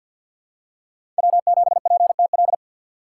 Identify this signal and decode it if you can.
Morse code, heard as a keyed tone, about 52 wpm, 700 Hz. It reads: W7PTL
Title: USA Call Signs - 5 Characters - 6x Character Spacing - 50wpm